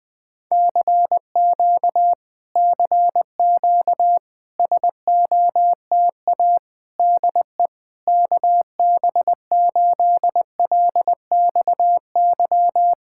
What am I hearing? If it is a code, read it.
Morse code, heard as a keyed tone, about 20 wpm, 700 Hz. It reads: CQ CQ SOTA DE KB8LXY
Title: Speed-Racing Common QSO Phrases - 20wpm